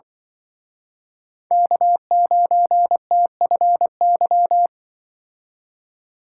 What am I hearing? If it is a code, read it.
Morse code, heard as a keyed tone, about 24 wpm, 700 Hz. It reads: K9TFY
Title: Speed Racing - USA 1x3 Call Signs 20wpm